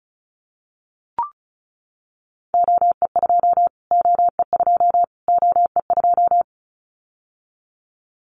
Morse code, heard as a keyed tone, about 34 wpm, 700 Hz. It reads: OE2 OE2 OE2
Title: Callsign Prefixes for Top 25 DXCC - Repeated 3x - 35wpm